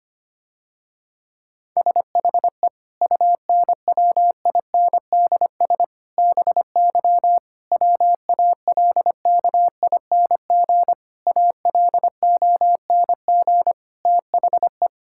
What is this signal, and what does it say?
Morse code, heard as a keyed tone, about 25 wpm, 700 Hz. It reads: SHE UNWINDS BY WALKING ALONG THE
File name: practice / 25wpm / Difficult Sentences with Hints - 25wpm